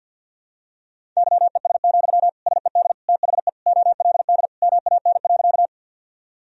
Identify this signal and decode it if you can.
Morse code, heard as a keyed tone, about 53 wpm, 700 Hz. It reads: YES, SED THE OLD MAN.
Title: Sentences from Top 500 Words and Abbreviations 50wpm